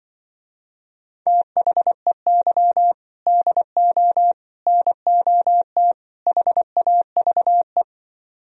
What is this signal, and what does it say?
Morse code, heard as a keyed tone, about 24 wpm, 700 Hz. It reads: THEY DO NOT HAVE
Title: Speed-Racing - Sets of 4 Words 20wpm